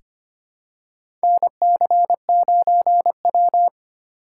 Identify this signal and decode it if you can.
Morse code, heard as a keyed tone, about 25 wpm, 700 Hz. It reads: NC9W
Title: K1USN SST QSO Practice - 25wpm